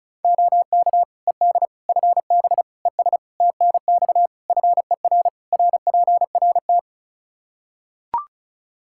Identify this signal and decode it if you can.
Morse code, heard as a keyed tone, about 36 wpm, 700 Hz. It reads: OK ED FB ES TNX FER RPRT
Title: K1USN SST QSO Practice 35wpm